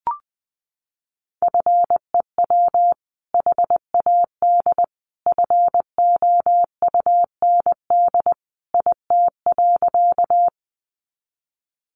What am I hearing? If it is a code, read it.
Morse code, heard as a keyed tone, about 20 wpm, 700 Hz. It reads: FEW HAD FOUND IT.
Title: Sentences from Top 300 Words 20wpm